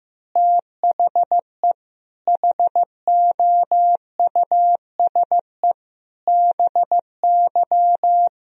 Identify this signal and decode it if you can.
Morse code, heard as a keyed tone, about 15 wpm, 700 Hz. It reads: THE HOUSE BY
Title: Sentences from Top 1000 Words - Review Part 4 - 15wpm